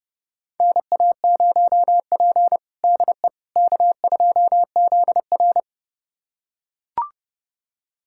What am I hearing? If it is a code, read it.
Morse code, heard as a keyed tone, about 30 wpm, 700 Hz. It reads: NA0P DE K2ZR
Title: USA Callsign DE Callsign - 4 Characters - 30wpm